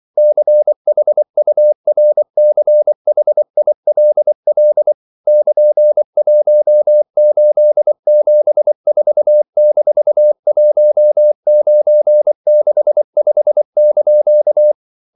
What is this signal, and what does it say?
Morse code, heard as a keyed tone, about 24 wpm, 600 Hz. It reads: CHURCHILL <KN>1874-1965)
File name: CW / Practice / EnglishQuotations0424